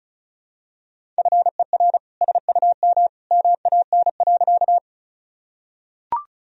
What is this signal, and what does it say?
Morse code, heard as a keyed tone, about 34 wpm, 690 Hz. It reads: FER SUM MAN.
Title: Sentences from Top 200 Words and Abbreviations 35wpm